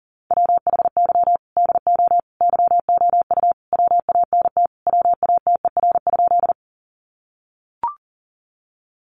Morse code, heard as a keyed tone, about 40 wpm, 700 Hz. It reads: WHY DO YOU WANT WATER?
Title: Sentences from Top 200 Words 40wpm